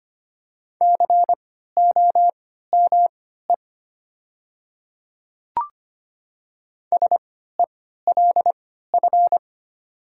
Morse code, heard as a keyed tone, about 25 wpm, spaced as for 14 wpm, 700 Hz. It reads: COME SELF
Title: 3x Character Spacing - Top 300 Words - 25wpm